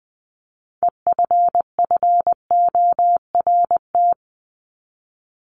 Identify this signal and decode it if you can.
Morse code, heard as a keyed tone, about 20 wpm, 700 Hz. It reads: EFFORT
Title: Top 1001-1250 Words 20wpm